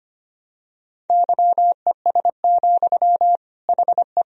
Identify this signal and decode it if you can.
Morse code, heard as a keyed tone, about 25 wpm, 700 Hz. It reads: YES, HE
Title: Sentences from Top 1000 Words - Review Part 4 - 25wpm